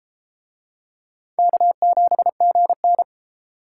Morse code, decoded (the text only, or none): K7GN